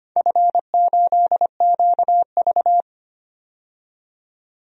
Morse code, heard as a keyed tone, about 25 wpm, 700 Hz. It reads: F8QV
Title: DXCC-100 Call Signs - 4 Characters - 25wpm